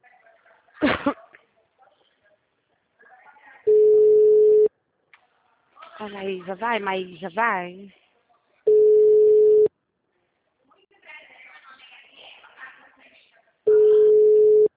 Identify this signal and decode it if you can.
Ringback tone (Europe, 425 Hz), with speech.